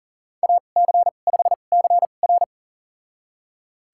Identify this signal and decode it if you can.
Morse code, heard as a keyed tone, about 40 wpm, spaced as for 29 wpm, 700 Hz. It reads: AC5CR
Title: USA Call Signs - 5 Characters - 2x Character Spacing - 40wpm